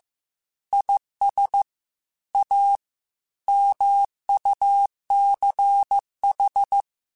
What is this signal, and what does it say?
Morse code, heard as a keyed tone, about 15 wpm, 800 Hz. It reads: IS A MUCH
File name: Technical Subjects / Technical Subjects-mp3s / Gates Intro @13